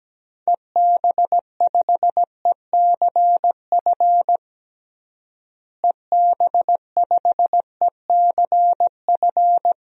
Morse code, heard as a keyed tone, about 17 wpm, 700 Hz. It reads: EB5ECF EB5ECF
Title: DXCC-100 Call Signs - 6 Characters - Repeated 2x - 17wpm